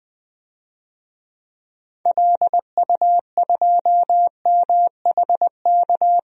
Morse code, heard as a keyed tone, about 20 wpm, 700 Hz. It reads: LU2MHK
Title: DXCC-100 Call Signs - 6 Characters - Repeated 3x - 20wpm